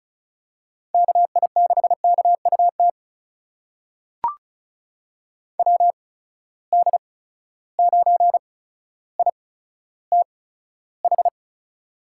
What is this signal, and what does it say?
Morse code, heard as a keyed tone, about 34 wpm, spaced as for 15 wpm, 700 Hz. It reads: KI6KUT WD9ITH
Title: USA Call Signs - 8x Character Spacing - 35wpm